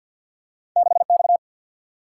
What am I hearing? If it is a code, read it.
Morse code, heard as a keyed tone, about 47 wpm, 700 Hz. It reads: 6X
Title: Single-Letter Single-Number 50wpm